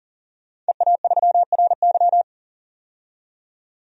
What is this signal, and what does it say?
Morse code, heard as a keyed tone, about 40 wpm, 700 Hz. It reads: EA3RY
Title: DXCC-100 Call Signs - 5 Characters - 40wpm